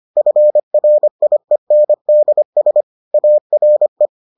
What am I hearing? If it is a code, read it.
Morse code, heard as a keyed tone, about 25 wpm, 600 Hz. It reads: FRIENDS ARE